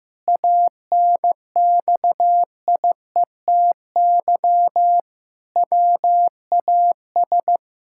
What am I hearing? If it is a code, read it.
Morse code, heard as a keyed tone, about 15 wpm, 700 Hz. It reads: ANXIETY WAS
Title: Sentences from Top 1600 Words 15wpm